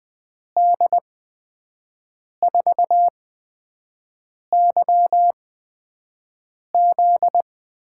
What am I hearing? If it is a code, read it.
Morse code, heard as a keyed tone, about 20 wpm, spaced as for 5 wpm, 700 Hz. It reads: D4YZ
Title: USA Call Signs - 5 Characters - 8x Character Spacing - 20wpm